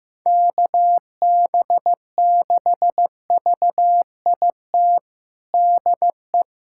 Morse code, heard as a keyed tone, about 15 wpm, 700 Hz. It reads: KB6VIT DE
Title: USA Callsign DE Callsign - 6 Characters - 15wpm